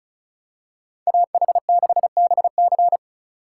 Morse code, heard as a keyed tone, about 36 wpm, 700 Hz. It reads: AH6BC
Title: USA Call Signs - 5 Characters - 2x Character Spacing - 35wpm